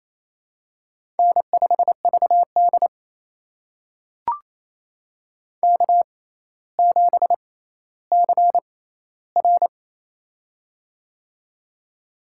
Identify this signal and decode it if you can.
Morse code, heard as a keyed tone, about 28 wpm, spaced as for 14 wpm, 700 Hz. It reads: N5VD K7CR
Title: USA Call Signs - 4 Characters - 6x Character Spacing - 28wpm